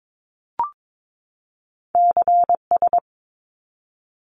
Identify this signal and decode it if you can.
Morse code, heard as a keyed tone, about 22 wpm, 700 Hz. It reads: CS
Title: Abbreviations 22wpm